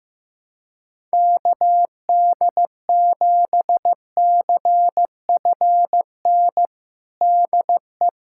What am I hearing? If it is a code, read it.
Morse code, heard as a keyed tone, about 15 wpm, 700 Hz. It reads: KD7CFN DE
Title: USA Callsign DE Callsign 15wpm